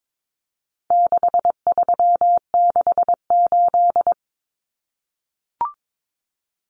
Morse code, heard as a keyed tone, about 22 wpm, 700 Hz. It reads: 6368
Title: Common Cut Numbers - 4-Digit 22wpm